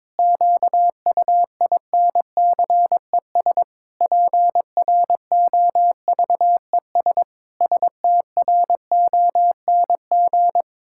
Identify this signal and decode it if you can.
Morse code, heard as a keyed tone, about 22 wpm, 700 Hz. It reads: QUINCES PROVES STRONG
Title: Pangram Challenge v2 - 22wpm